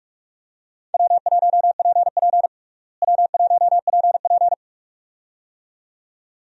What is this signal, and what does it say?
Morse code, heard as a keyed tone, about 46 wpm, 700 Hz. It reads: W1PP W1PP
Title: USA Call Signs - 4 Characters - Repeated 2x - 45wpm